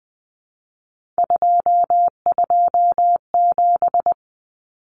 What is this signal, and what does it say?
Morse code, heard as a keyed tone, about 20 wpm, 700 Hz. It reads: 227
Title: Numbers - 3-Digit 20wpm